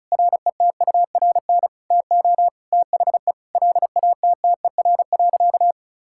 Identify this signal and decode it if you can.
Morse code, heard as a keyed tone, about 34 wpm, 700 Hz. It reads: RETURN TO THE LATTER.
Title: Sentences from Top 1600 Words 35wpm